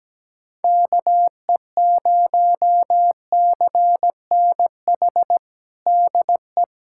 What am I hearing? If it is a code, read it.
Morse code, heard as a keyed tone, about 17 wpm, 700 Hz. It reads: KE0CNH DE
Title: USA Callsign DE Callsign - 6 Characters - 17wpm